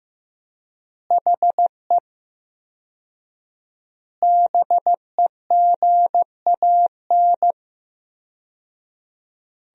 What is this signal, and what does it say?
Morse code, heard as a keyed tone, about 15 wpm, 700 Hz. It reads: HE BEGAN
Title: 4x Word-Spacing - Sentences from Top 300 Words - 15wpm